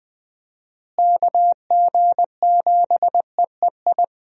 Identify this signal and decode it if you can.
Morse code, heard as a keyed tone, about 20 wpm, 700 Hz. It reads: KG7EEI
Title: USA Call Signs - 6 Characters - Repeated 2x - 20wpm